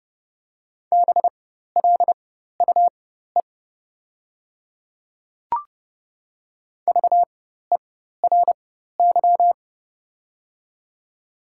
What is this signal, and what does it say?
Morse code, heard as a keyed tone, about 30 wpm, spaced as for 14 wpm, 690 Hz. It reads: BLUE VERY
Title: 4x Character Spacing - Top 500 Words - 30wpm